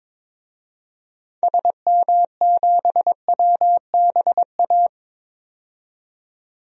Morse code, heard as a keyed tone, about 22 wpm, 700 Hz. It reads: SM7WBA
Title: DXCC-100 Call Signs - 6 Characters - 22wpm